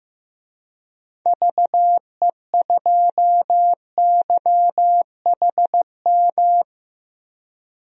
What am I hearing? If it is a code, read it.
Morse code, heard as a keyed tone, about 15 wpm, 700 Hz. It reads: VE2YHM
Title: DXCC-100 Call Signs - 6 Characters - Repeated 2x - 15wpm